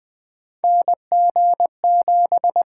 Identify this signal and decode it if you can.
Morse code, heard as a keyed tone, about 21 wpm, 700 Hz. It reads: NG7
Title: USA Call Signs - 4 Characters - 8x Character Spacing - 20wpm